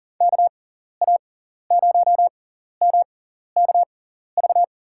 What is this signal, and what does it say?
Morse code, heard as a keyed tone, about 40 wpm, spaced as for 14 wpm, 700 Hz. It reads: KA0MKV
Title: USA Call Signs - 6 Characters - 6x Character Spacing - 40wpm